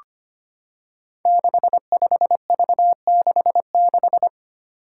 Morse code, heard as a keyed tone, about 25 wpm, 700 Hz. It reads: 65V66
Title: Five Challenging Letter-Number Combinations 25wpm